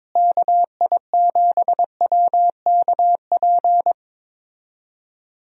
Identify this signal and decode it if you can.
Morse code, heard as a keyed tone, about 22 wpm, 700 Hz. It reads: KI7WKP
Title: USA Call Signs - 6 Characters - Repeated 2x - 22wpm